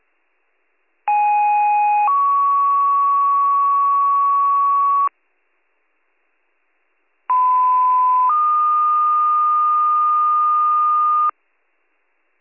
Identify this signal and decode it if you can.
Two-tone page, heard as a one-second tone, then a three-second tone, about 830 then 1120 Hz, about 1010 then 1220 Hz. It is heard from 1.1 to 11.3 s.